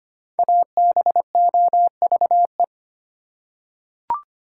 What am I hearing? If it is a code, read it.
Morse code, heard as a keyed tone, about 25 wpm, 700 Hz. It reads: ABOVE
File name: 5 Letter Words - First 3 Letters Given - 25wpm